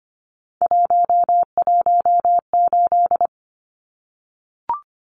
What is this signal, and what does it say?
Morse code, heard as a keyed tone, about 25 wpm, 690 Hz. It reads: 118
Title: Numbers - 3-Digit 25wpm